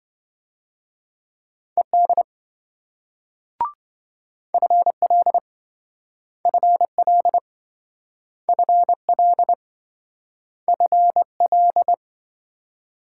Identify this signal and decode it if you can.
Morse code, heard as a keyed tone, about 27 wpm, 700 Hz. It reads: ED FL FL FL FL